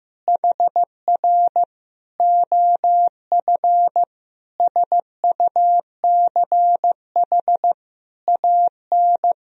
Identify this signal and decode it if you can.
Morse code, heard as a keyed tone, about 16 wpm, 700 Hz. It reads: HR OF SUCH AN
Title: Sentences from Top 400 Words and Abbreviations 15wpm